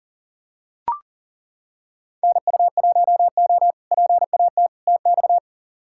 Morse code, heard as a keyed tone, about 40 wpm, 700 Hz. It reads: NU1O PAT TX
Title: K1USN SST QSO Practice - 40wpm